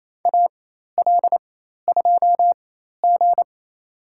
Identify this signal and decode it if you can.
Morse code, heard as a keyed tone, about 28 wpm, spaced as for 13 wpm, 700 Hz. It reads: AL2G
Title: USA Call Signs - 4 Characters - 4x Character Spacing - 28wpm